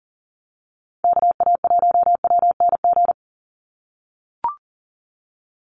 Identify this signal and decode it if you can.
Morse code, heard as a keyed tone, about 40 wpm, 700 Hz. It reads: KA1WNG
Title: USA Call Signs 40wpm